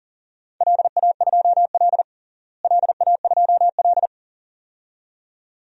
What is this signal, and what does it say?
Morse code, heard as a keyed tone, about 40 wpm, 700 Hz. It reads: LA2L LA2L
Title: DXCC-100 Call Signs - 4 Characters - Repeated 2x - 40wpm